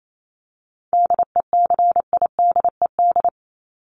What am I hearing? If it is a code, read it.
Morse code, heard as a keyed tone, about 28 wpm, 700 Hz. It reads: DECIDED
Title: ED Suffix - Top 100 Words with Base Words 28wpm